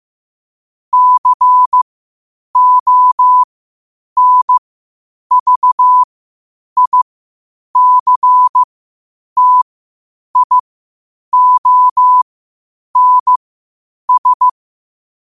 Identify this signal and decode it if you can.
Morse code, heard as a keyed tone, about 15 wpm, spaced as for 8 wpm, 1000 Hz. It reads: CONVICTIONS
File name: qotdfiles / 8wpm